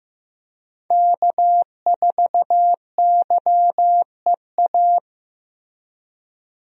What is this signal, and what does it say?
Morse code, heard as a keyed tone, about 15 wpm, 700 Hz. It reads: K4YEA
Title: USA Call Signs - 5 Characters - Repeated 3x - 15wpm